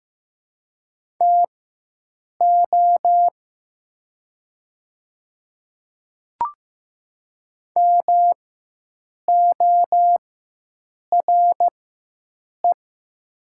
Morse code, heard as a keyed tone, about 15 wpm, spaced as for 7 wpm, 700 Hz. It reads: TO MORE
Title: 4x Character Spacing - Top 100 Words - 15wpm